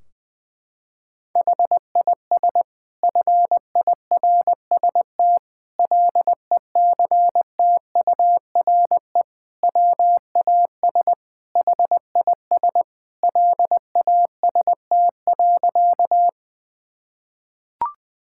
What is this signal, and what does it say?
Morse code, heard as a keyed tone, about 20 wpm, 700 Hz. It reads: HIS FIRST LECTURE WAS HIS LAST.
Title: Sentences from Top 1600 Words 20wpm